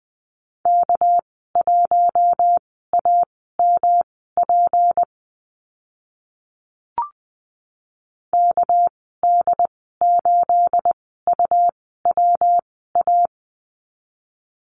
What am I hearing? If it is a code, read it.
Morse code, heard as a keyed tone, about 20 wpm, spaced as for 14 wpm, 700 Hz. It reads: K1AMP KD8UWA
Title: 213 Lesson 34g - Number 8 - Call Signs 20wpm